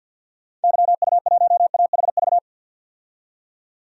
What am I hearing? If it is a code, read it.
Morse code, heard as a keyed tone, about 50 wpm, 700 Hz. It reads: YU1AHV
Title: DXCC-100 Call Signs - 6 Characters - 50wpm